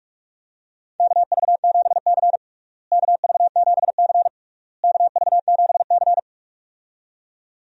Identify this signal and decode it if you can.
Morse code, heard as a keyed tone, about 46 wpm, 700 Hz. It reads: KV7C KV7C KV7C
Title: USA Call Signs - 4 Characters - Repeated 3x - 45wpm